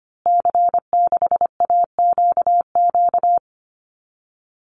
Morse code, heard as a keyed tone, about 25 wpm, 700 Hz. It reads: C6AQQ
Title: Call Signs Repeated 2x - 25wpm